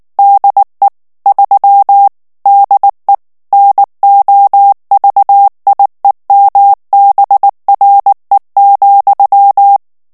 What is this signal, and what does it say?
Morse code, heard as a keyed tone, about 19 wpm, 800 Hz. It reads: DE 3 DE NOVIEMBRE,
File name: ESPAGNOL TEXTE EN CLAIR 2 VITESSE 18 MOTS MINUTES